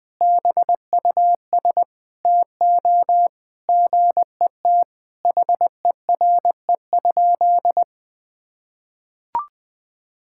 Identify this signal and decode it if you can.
Morse code, heard as a keyed tone, about 21 wpm, 700 Hz. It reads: BUS TO GET HERE?
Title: Sentences from Top 1000 Words - Review Part 6 - 20wpm